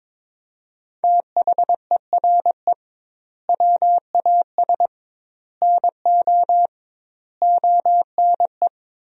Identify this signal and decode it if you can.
Morse code, heard as a keyed tone, about 22 wpm, 700 Hz. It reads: THERE WAS NO ONE
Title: 2x Word-Spacing - Sentences from Top 200 Words - 22wpm